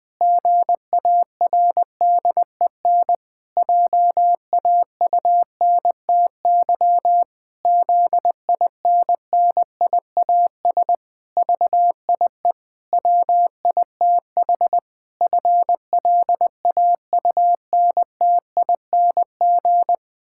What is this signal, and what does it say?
Morse code, heard as a keyed tone, about 20 wpm, 700 Hz. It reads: GARDEN JAUNTY ZINNIAS VIE WITH FLAUNTING
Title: Pangram Challenge v2 - 20wpm